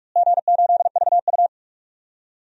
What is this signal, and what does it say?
Morse code, heard as a keyed tone, about 46 wpm, 700 Hz. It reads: G8VU
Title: DXCC-100 Call Signs - 4 Characters - Repeated 2x - 45wpm